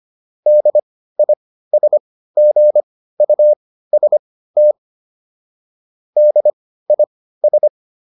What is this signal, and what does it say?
Morse code, heard as a keyed tone, about 25 wpm, spaced as for 15 wpm, 600 Hz. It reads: DISGUST DIS